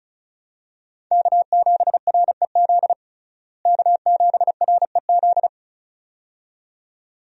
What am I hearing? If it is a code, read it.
Morse code, heard as a keyed tone, about 36 wpm, 700 Hz. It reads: K7REZ K7REZ
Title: USA Call Signs - 5 Characters - Repeated 2x - 35wpm